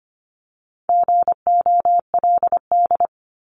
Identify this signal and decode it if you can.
Morse code, heard as a keyed tone, about 25 wpm, 700 Hz. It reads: GOLD
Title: Top 500 Words 25wpm